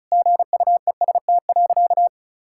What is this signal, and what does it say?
Morse code, heard as a keyed tone, about 36 wpm, 700 Hz. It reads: GUEST.
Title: Sentences from Top 1600 Words 35wpm